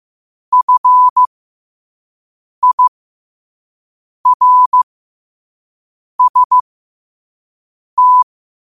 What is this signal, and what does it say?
Morse code, heard as a keyed tone, about 15 wpm, spaced as for 5 wpm, 1000 Hz. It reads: FIRST
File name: qotdfiles / 5wpm